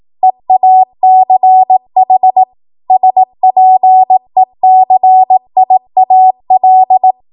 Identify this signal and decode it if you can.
Morse code, heard as a keyed tone, about 18 wpm, 750 Hz. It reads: EACH SPECIAL
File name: Morse / 18WPM